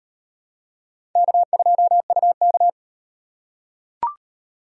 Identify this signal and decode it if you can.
Morse code, heard as a keyed tone, about 38 wpm, 700 Hz. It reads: K2UK